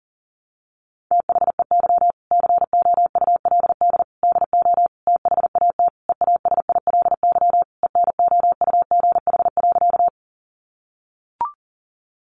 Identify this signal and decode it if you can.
Morse code, heard as a keyed tone, about 37 wpm, 700 Hz. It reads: THEY COULD DO THAT EASILY ENOUGH.
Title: Sentences from Top 900 Words 40wpm